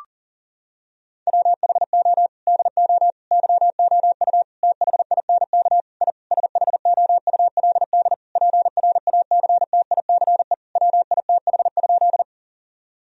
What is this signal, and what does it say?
Morse code, heard as a keyed tone, about 40 wpm, 700 Hz. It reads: WHO DO YOU THINK I SHOULD PRACTICE WITH?
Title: Sentences from Top 1000 Words - Review Part 4 - 40wpm